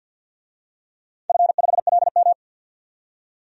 Morse code, heard as a keyed tone, about 53 wpm, 700 Hz. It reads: F5LM
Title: DXCC-100 Call Signs - 4 Characters - 50wpm